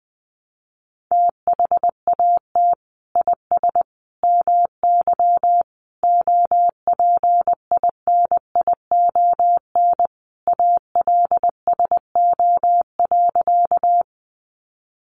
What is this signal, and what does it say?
Morse code, heard as a keyed tone, about 20 wpm, 700 Hz. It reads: THAT IS MY OPINION ALSO.
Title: Sentences from Top 1200 Words 20wpm